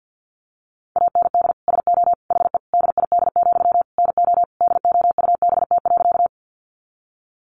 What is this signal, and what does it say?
Morse code, heard as a keyed tone, about 50 wpm, 700 Hz. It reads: AND SO HE DID, NO DOUBT.
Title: Sentences from Top 1500 Words 50wpm